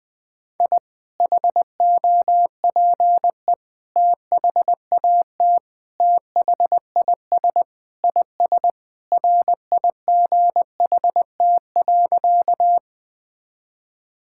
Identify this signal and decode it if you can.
Morse code, heard as a keyed tone, about 21 wpm, 700 Hz. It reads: I HOPE THAT THIS IS RIGHT.
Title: Sentences with 3-Word Phrases - 20wpm